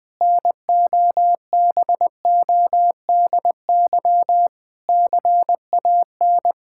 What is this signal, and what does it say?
Morse code, heard as a keyed tone, about 20 wpm, 700 Hz. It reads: NOBODY CAN
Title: Sentences from Top 1600 Words 20wpm